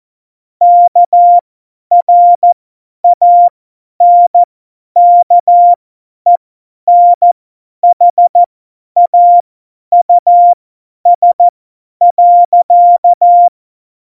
Morse code, heard as a keyed tone, about 14 wpm, spaced as for 10 wpm, 700 Hz. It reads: KRANKENHAUS.
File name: Morsekurs / Witze / Witz 06. Februar 2018, 50 F70